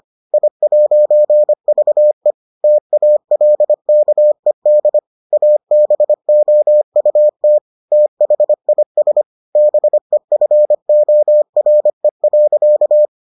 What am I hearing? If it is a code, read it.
Morse code, heard as a keyed tone, about 25 wpm, 600 Hz. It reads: I'VE TALKED ABOUT THIS BEFORE.